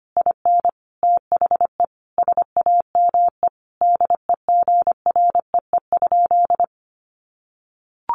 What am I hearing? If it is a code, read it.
Morse code, heard as a keyed tone, about 25 wpm, 700 Hz. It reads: IN THE SAME DEGREE?
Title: Sentences from Top 1500 Words 25wpm